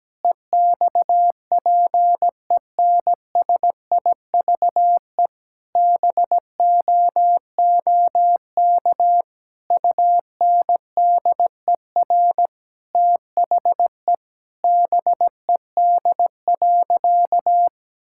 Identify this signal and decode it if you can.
Morse code, heard as a keyed tone, about 17 wpm, 700 Hz. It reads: EXPENSIVE BOOK UNDER THE BED.
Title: Sentences from Top 1000 Words - Review Part 6 - 17wpm